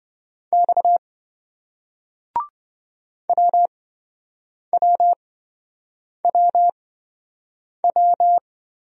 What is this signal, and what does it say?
Morse code, heard as a keyed tone, about 26 wpm, spaced as for 7 wpm, 700 Hz. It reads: X WWWW